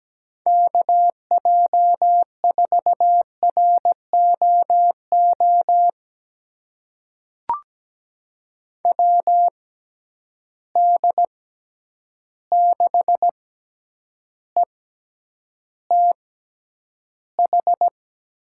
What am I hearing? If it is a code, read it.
Morse code, heard as a keyed tone, about 17 wpm, spaced as for 9 wpm, 700 Hz. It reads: KJ4ROO WD6ETH
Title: USA Call Signs - 6 Characters - 6x Character Spacing - 17wpm